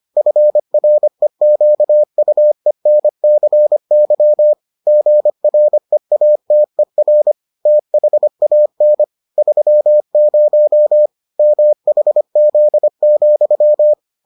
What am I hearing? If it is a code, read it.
Morse code, heard as a keyed tone, about 25 wpm, 600 Hz. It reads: FREQUENCY GREATER THAN 30 MHZ,